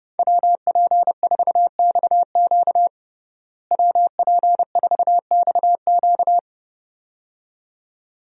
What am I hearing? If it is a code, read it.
Morse code, heard as a keyed tone, about 30 wpm, 700 Hz. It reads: WP4XQ WP4XQ
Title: USA Call Signs - 5 Characters - Repeated 2x - 30wpm